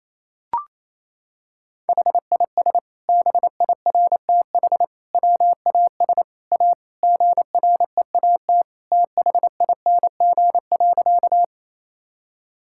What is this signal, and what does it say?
Morse code, heard as a keyed tone, about 28 wpm, 700 Hz. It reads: HIS BIRTH WAS A GREAT THING.
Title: Sentences from Top 1000 Words - Review Part 5 - 28wpm